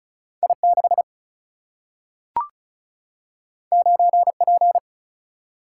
Morse code, heard as a keyed tone, about 36 wpm, 700 Hz. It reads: I6 9P
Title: Single-Letter Single-Number 35wpm